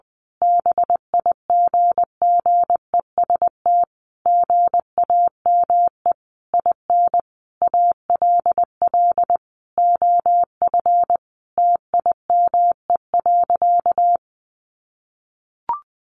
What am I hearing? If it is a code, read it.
Morse code, heard as a keyed tone, about 20 wpm, 700 Hz. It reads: BIGGEST GAME IN ALL OF TIME.
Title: A Sound Of Thunder 20wpm